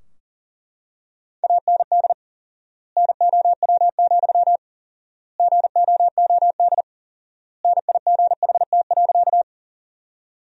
Morse code, heard as a keyed tone, about 40 wpm, 700 Hz. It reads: AND NOW, GOOD NIGHT.
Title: 4x Word-Spacing - Sentences from Top 300 Words - 40wpm